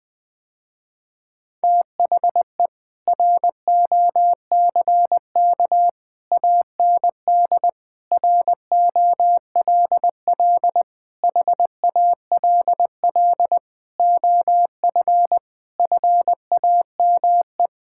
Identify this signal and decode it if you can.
Morse code, heard as a keyed tone, about 21 wpm, 700 Hz. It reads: THE ROCK AND ROLL HALL OF FAME
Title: Sentences with 7-Word Phrases - Spoken Beginning - 20wpm